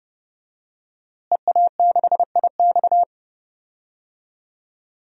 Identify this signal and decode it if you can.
Morse code, heard as a keyed tone, about 31 wpm, 700 Hz. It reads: EA6IX
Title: DXCC-100 Call Signs - 5 Characters - Repeated 3x - 30wpm